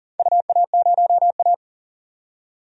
Morse code, heard as a keyed tone, about 40 wpm, 700 Hz. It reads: UA0A
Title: DXCC-100 Call Signs - 4 Characters - 40wpm